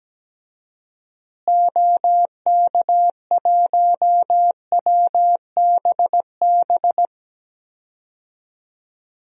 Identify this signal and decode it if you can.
Morse code, heard as a keyed tone, about 17 wpm, 700 Hz. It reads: OK1WBB
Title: DXCC-100 Call Signs - 6 Characters - Repeated 2x - 17wpm